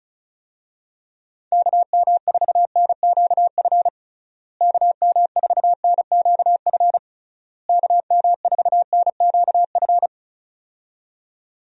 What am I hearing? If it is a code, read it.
Morse code, heard as a keyed tone, about 34 wpm, 700 Hz. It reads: KM4NQF KM4NQF KM4NQF
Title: USA Call Signs - 6 Characters - Repeated 3x - 35wpm